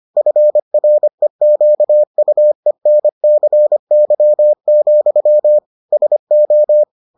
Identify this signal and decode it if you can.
Morse code, heard as a keyed tone, about 25 wpm, 600 Hz. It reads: FREQUENCY, SO